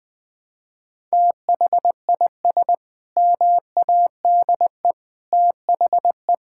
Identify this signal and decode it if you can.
Morse code, heard as a keyed tone, about 20 wpm, 700 Hz. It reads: THIS MADE THE
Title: Sentences from Top 1000 Words - Review Part 6 - 20wpm